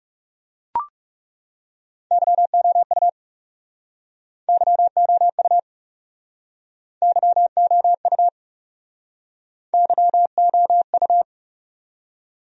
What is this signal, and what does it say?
Morse code, heard as a keyed tone, about 37 wpm, 700 Hz. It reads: YOU YOU YOU YOU
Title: Speed-Racing Top 500 Words 30wpm